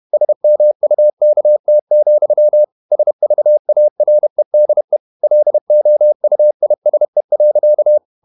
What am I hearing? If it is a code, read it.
Morse code, heard as a keyed tone, about 31 wpm, 600 Hz. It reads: SMUKT, SVAREDE LOUISE.
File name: Ferie QRQ 31wpm